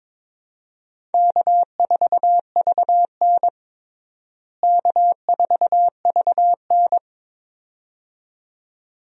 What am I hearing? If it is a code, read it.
Morse code, heard as a keyed tone, about 22 wpm, 700 Hz. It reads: K4VN K4VN
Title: USA Call Signs - 4 Characters - Repeated 2x - 22wpm